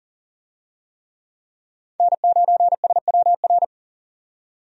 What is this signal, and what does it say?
Morse code, heard as a keyed tone, about 39 wpm, 700 Hz. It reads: N9SWR
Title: USA Call Signs - 5 Characters - 40wpm